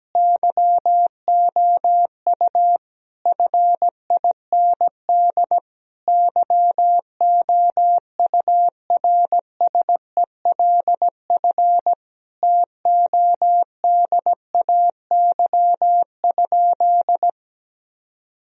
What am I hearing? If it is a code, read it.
Morse code, heard as a keyed tone, about 17 wpm, 700 Hz. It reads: YOU FIND YOURSELF TODAY?
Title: Sentences from Top 1000 Words - Review Part 4 - 17wpm